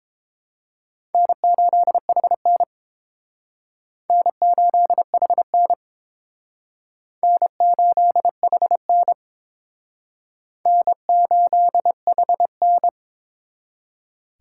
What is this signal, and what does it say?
Morse code, heard as a keyed tone, about 28 wpm, 700 Hz. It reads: N8HN N8HN N8HN N8HN
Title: Speed Racing - USA 1x2 Call Signs 22wpm